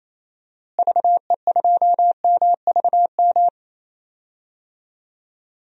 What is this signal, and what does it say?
Morse code, heard as a keyed tone, about 28 wpm, 700 Hz. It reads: VE2MVM
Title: DXCC-100 Call Signs - 6 Characters - Repeated 2x - 28wpm